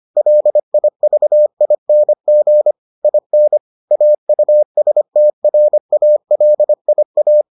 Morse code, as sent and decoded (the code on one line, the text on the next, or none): .-.. .. ...- .. -. --. / .. -. / .- ..- ... - .-. .- .-.. .. .-
LIVING IN AUSTRALIA